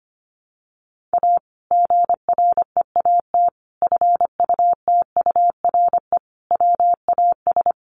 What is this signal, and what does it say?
Morse code, heard as a keyed tone, about 25 wpm, 700 Hz. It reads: A GREAT FUTURE WAS
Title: Sentences from Top 700 Words 25wpm